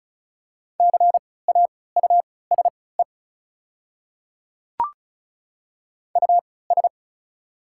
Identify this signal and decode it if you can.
Morse code, heard as a keyed tone, about 34 wpm, spaced as for 20 wpm, 700 Hz. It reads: CAUSE US
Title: 3x Character Spacing - Top 200 Words - 35wpm